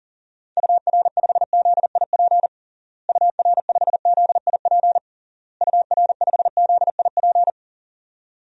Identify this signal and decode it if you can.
Morse code, heard as a keyed tone, about 40 wpm, 700 Hz. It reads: UR5ZIP UR5ZIP UR5ZIP
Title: DXCC-100 Call Signs - 6 Characters - Repeated 3x - 40wpm